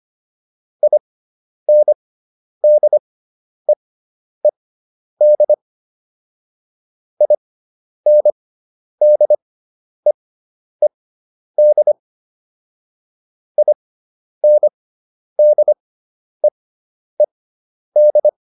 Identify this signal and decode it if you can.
Morse code, heard as a keyed tone, about 25 wpm, spaced as for 10 wpm, 600 Hz. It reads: INDEED INDEED INDEED